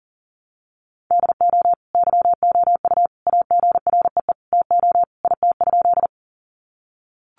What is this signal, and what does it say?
Morse code, heard as a keyed tone, about 40 wpm, 700 Hz. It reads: DO YOU AGREE TO IT?
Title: Sentences from Top 1100 Words 40wpm